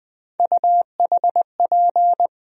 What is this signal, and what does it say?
Morse code, heard as a keyed tone, about 20 wpm, 700 Hz. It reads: UHP
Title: USA Call Sign Postfixes - 3 Letters 20wpm